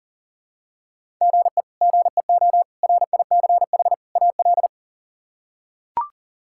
Morse code, heard as a keyed tone, about 40 wpm, 700 Hz. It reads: GE GEO RICH AL
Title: K1USN SST QSO Practice - 40wpm